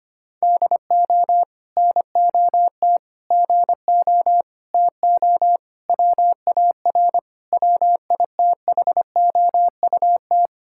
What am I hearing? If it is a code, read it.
Morse code, heard as a keyed tone, about 25 wpm, 700 Hz. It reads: DO NOT GO TO WAR WITHOUT
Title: Sentences from Top 1000 Words - Review Part 5 - 25wpm